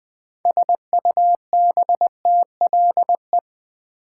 Morse code, heard as a keyed tone, about 20 wpm, 700 Hz. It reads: SUBTLE
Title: Build A Word - 2 Letters At A Time - 6-Letter Words - 20wpm